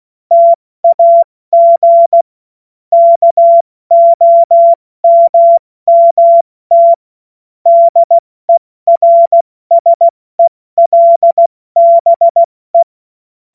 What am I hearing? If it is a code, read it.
Morse code, heard as a keyed tone, about 16 wpm, 680 Hz. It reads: TAG KOMMT DERSELBE
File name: Morsekurs / Witze / Witz 15. November 2022 70 F80